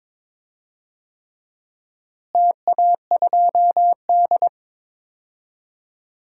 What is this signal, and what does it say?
Morse code, heard as a keyed tone, about 22 wpm, 700 Hz. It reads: TA2D
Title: DXCC-100 Call Signs - 4 Characters - Repeated 3x - 22wpm